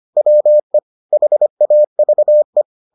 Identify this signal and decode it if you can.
Morse code, heard as a keyed tone, about 25 wpm, 600 Hz. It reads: WE HAVE